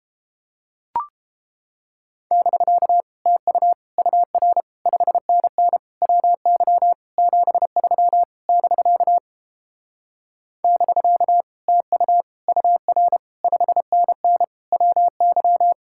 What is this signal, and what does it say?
Morse code, heard as a keyed tone, about 29 wpm, 700 Hz. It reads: <BK> TU UR 5NN WY 73 <BK> <BK> TU UR 5NN WY